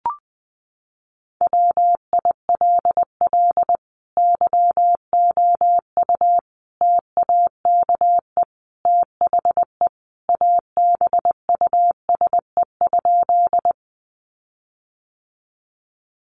Morse code, heard as a keyed tone, about 19 wpm, 700 Hz. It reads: WILL YOU TAKE THE ABUSE?
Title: Sentences from Top 1300 Words 20wpm